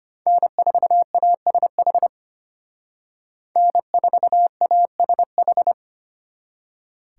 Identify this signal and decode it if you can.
Morse code, heard as a keyed tone, about 33 wpm, 700 Hz. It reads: N4ASH N4ASH